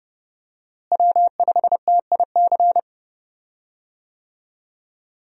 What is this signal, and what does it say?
Morse code, heard as a keyed tone, about 31 wpm, 700 Hz. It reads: W5TIC
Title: USA Call Signs - 5 Characters - Repeated 2x - 30wpm